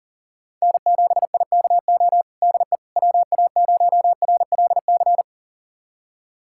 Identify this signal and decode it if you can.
Morse code, heard as a keyed tone, about 37 wpm, 700 Hz. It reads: N7IKO DE WA0RLC
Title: USA Callsign DE Callsign 40wpm